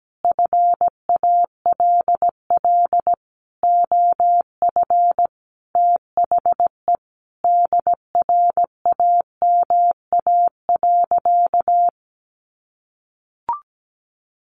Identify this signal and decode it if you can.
Morse code, heard as a keyed tone, about 17 wpm, 700 Hz. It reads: FALL OF THE DRAMA.
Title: Sentences from Top 1500 Words 17wpm